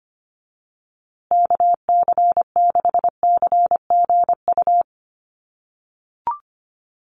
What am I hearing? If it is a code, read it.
Morse code, heard as a keyed tone, about 25 wpm, 700 Hz. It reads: KC6CGU
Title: USA Call Signs Repeated 2x 25wpm